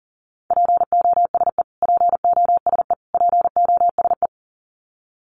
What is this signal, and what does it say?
Morse code, heard as a keyed tone, about 40 wpm, 700 Hz. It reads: POSE POSE POSE
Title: Top 500 Words Repeated 40wpm